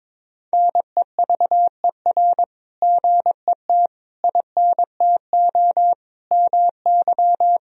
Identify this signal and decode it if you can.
Morse code, heard as a keyed tone, about 22 wpm, 700 Hz. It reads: NEVER GET INTO MY
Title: Sentences from Top 1000 Words - Review Part 3 - 22wpm